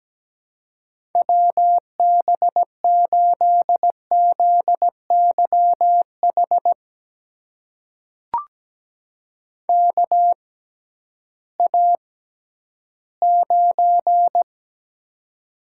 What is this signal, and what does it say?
Morse code, heard as a keyed tone, about 17 wpm, spaced as for 11 wpm, 700 Hz. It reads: WB8ZYH KA9
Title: USA Call Signs - 6 Characters - 6x Character Spacing - 17wpm